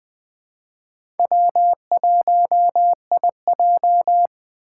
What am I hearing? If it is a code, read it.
Morse code, heard as a keyed tone, about 21 wpm, 700 Hz. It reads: W1IJ
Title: USA Call Signs - 6x Character Spacing - 20wpm